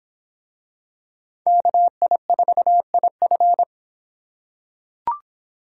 Morse code, heard as a keyed tone, about 26 wpm, 700 Hz. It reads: KI4IF